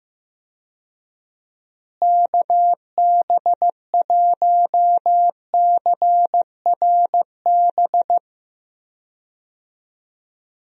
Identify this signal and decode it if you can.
Morse code, heard as a keyed tone, about 15 wpm, 700 Hz. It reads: KB1CRB
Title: USA Call Signs - 6 Characters - Repeated 2x - 15wpm